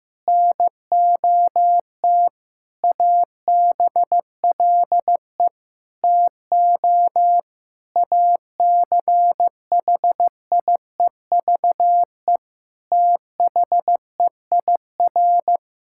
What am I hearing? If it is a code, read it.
Morse code, heard as a keyed tone, about 16 wpm, 700 Hz. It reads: NOT ABLE TO ACHIEVE THEIR
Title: Sentences from Top 1000 Words - Review Part 6 - 15wpm